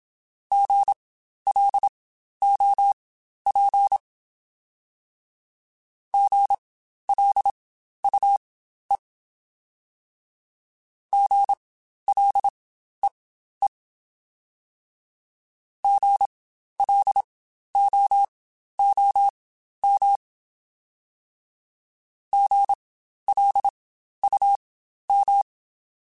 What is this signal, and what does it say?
Morse code, heard as a keyed tone, about 27 wpm, spaced as for 12 wpm, 800 Hz. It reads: GLOP GLUE GLEE GLOOM GLUM
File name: Spaced Alphas / Spaced Alphas mp3s / Gs sp @25